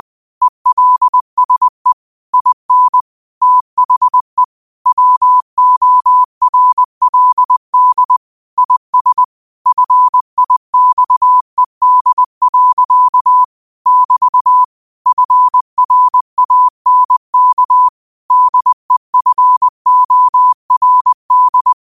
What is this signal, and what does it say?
Morse code, heard as a keyed tone, about 20 wpm, 1000 Hz. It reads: ELSE IN THE WORLD IS FIXED. <BT> FRANK DEFORD
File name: qotdfiles / 20wpm